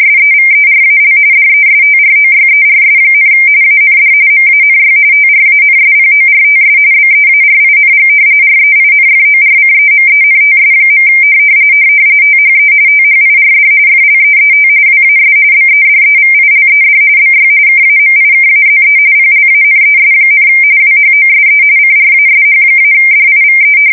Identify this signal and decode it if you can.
RTTY, about 45.45 baud. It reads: SEVERAL DESTROYERS, AND MANY TRANSPORTS BEARING REINFORCEMENTS. THE THIRD FLEET WAS BEDEVILED BY KAMIKAZE PLANES, WHOSE CRASH-DIVE
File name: fsk170 / battleshipnj